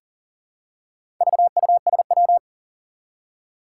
Morse code, heard as a keyed tone, about 38 wpm, 700 Hz. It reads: VUSW
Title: Four Challenging Letter-Number Combinations 40wpm